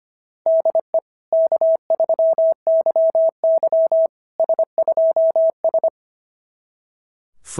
Morse code, heard as a keyed tone, about 25 wpm, 690 Hz. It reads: DE K3YY S2S
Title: SOTA QSO Practice 25wpm